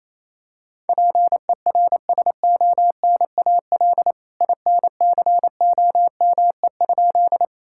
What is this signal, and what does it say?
Morse code, heard as a keyed tone, about 28 wpm, 700 Hz. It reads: PERSONAL INCOME?
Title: Sentences from Top 1000 Words - Review Part 5 - 28wpm